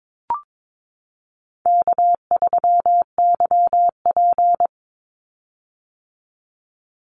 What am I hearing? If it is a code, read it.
Morse code, heard as a keyed tone, about 22 wpm, 700 Hz. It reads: K3YP
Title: Call Signs - 22wpm